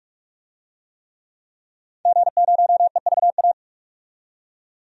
Morse code, heard as a keyed tone, about 42 wpm, 700 Hz. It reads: G0EVA
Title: DXCC-100 Call Signs - 5 Characters - 45wpm